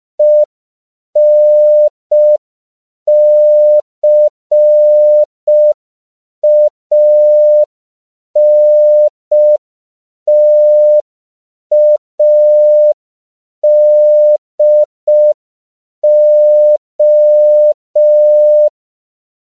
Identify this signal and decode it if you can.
Morse code, heard as a keyed tone, about 5 wpm, 600 Hz. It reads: ENCANTADO
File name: CW / 5PPM